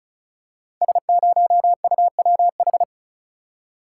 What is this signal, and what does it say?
Morse code, heard as a keyed tone, about 36 wpm, 690 Hz. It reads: S0UWH
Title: Code Groups - 5 Letter:Numbers - 35wpm